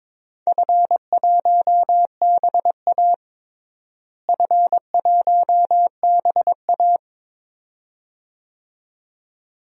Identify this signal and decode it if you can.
Morse code, heard as a keyed tone, about 21 wpm, 700 Hz. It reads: F1BA F1BA
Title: DXCC-100 Call Signs - 4 Characters - Repeated 2x - 22wpm